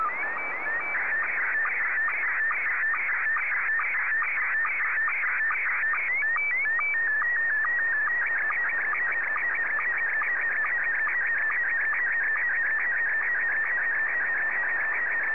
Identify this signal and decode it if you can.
SSTV (slow-scan television), Scottie 1.
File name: sstv / S1-VK3DEK